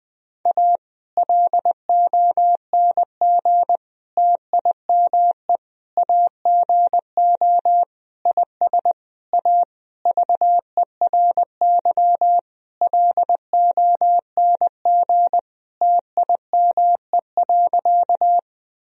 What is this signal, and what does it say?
Morse code, heard as a keyed tone, about 20 wpm, 700 Hz. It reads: A LONG TIME AGO IS A VERY LONG TIME.
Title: Sentences from Top 1000 Words - Review Part 3 - 20wpm